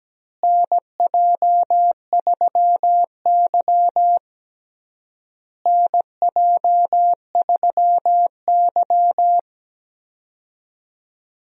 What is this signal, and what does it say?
Morse code, heard as a keyed tone, about 17 wpm, 700 Hz. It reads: NJ3Y NJ3Y
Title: USA Call Signs - 4 Characters - Repeated 3x - 17wpm